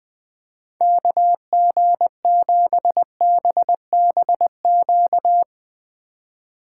Morse code, heard as a keyed tone, about 20 wpm, 700 Hz. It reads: KG7BBQ